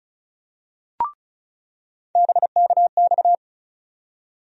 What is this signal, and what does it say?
Morse code, heard as a keyed tone, about 34 wpm, 700 Hz. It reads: BKX
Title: USA Call Sign Postfixes - 3 Letters 35wpm